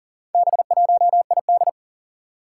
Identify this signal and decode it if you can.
Morse code, heard as a keyed tone, about 40 wpm, 700 Hz. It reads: B1ID
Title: Four Challenging Letter-Number Combinations 40wpm